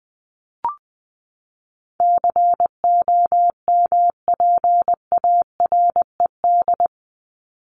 Morse code, heard as a keyed tone, about 20 wpm, 700 Hz. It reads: COMPARED
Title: ED Suffix - Top 100 Words with Base Words 20wpm